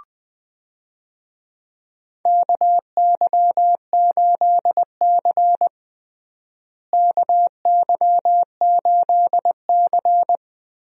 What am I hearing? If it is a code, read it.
Morse code, heard as a keyed tone, about 20 wpm, 700 Hz. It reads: KY8C KY8C
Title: USA Call Signs - 4 Characters - Repeated 3x - 20wpm